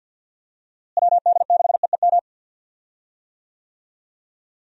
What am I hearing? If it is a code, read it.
Morse code, heard as a keyed tone, about 52 wpm, 700 Hz. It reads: WD6EEM
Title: USA Call Signs - 6 Characters - Repeated 2x - 50wpm